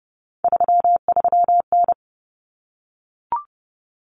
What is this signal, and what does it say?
Morse code, heard as a keyed tone, about 30 wpm, 700 Hz. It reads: 33N
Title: Common Cut Numbers - 3-Digit 30wpm